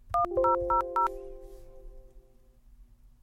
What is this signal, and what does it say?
DTMF, heard as telephone touch tones, tones of about 130 ms. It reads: 1000